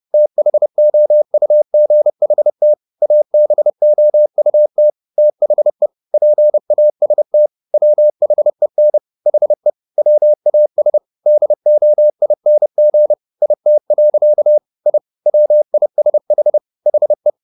Morse code, heard as a keyed tone, about 30 wpm, 600 Hz. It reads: THOUGHT ABOUT THE PAST WHEN HE WAS DOING IT. I WISH HE
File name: the old man and the sea / the old man and the sea 1 30wpm